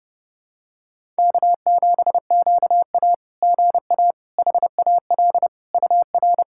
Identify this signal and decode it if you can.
Morse code, heard as a keyed tone, about 30 wpm, 690 Hz. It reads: K7QA GA HAL UR
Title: SOTA QSO Practice 30wpm